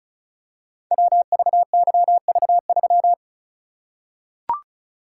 Morse code, heard as a keyed tone, about 36 wpm, 700 Hz. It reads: WVYV3
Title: Code Groups - 5 Letter:Numbers - 35wpm